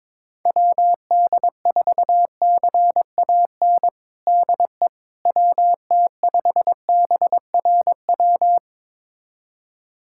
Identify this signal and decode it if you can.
Morse code, heard as a keyed tone, about 22 wpm, 700 Hz. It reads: WD4CAN DE WT5BRW
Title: USA Callsign DE Callsign 22wpm